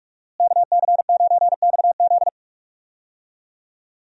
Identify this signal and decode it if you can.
Morse code, heard as a keyed tone, about 42 wpm, 700 Hz. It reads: KC9XZ
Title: USA Call Signs - 5 Characters - Repeated 2x - 45wpm